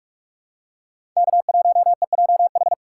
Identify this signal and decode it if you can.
Morse code, heard as a keyed tone, about 44 wpm, 700 Hz. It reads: K1EJH
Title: USA Call Signs - 5 Characters - 6x Character Spacing - 45wpm